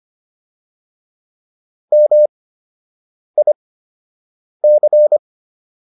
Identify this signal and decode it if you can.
Morse code, heard as a keyed tone, about 25 wpm, spaced as for 7 wpm, 600 Hz. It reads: MIC